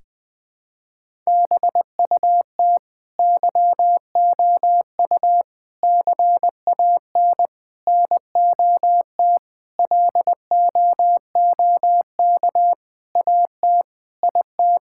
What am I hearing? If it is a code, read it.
Morse code, heard as a keyed tone, about 20 wpm, 700 Hz. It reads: BUT YOU CAN NOT LOOK AT IT
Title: Sentences from Top 1000 Words - Review Part 6 - 20wpm